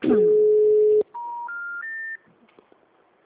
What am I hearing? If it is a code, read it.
Special information tone, with speech.